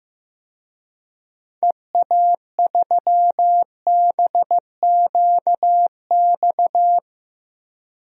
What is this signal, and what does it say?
Morse code, heard as a keyed tone, about 15 wpm, 700 Hz. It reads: EA3BQX
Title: DXCC-100 Call Signs - 6 Characters - 15wpm